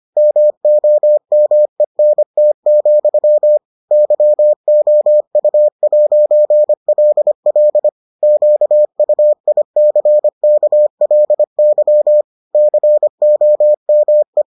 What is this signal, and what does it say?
Morse code, heard as a keyed tone, about 25 wpm, 600 Hz. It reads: MOMENT, YOU'LL QUICKLY COME